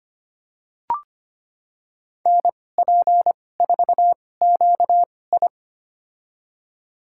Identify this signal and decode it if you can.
Morse code, heard as a keyed tone, about 25 wpm, spaced as for 18 wpm, 700 Hz. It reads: NP4QI
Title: USA Call Signs - 2x Character Spacing - 25wpm